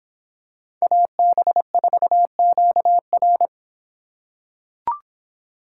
Morse code, heard as a keyed tone, about 25 wpm, 700 Hz. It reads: AB4QR